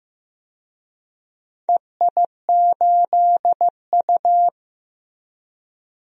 Morse code, heard as a keyed tone, about 15 wpm, 700 Hz. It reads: EI8U
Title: DXCC-100 Call Signs - 4 Characters - Repeated 2x - 15wpm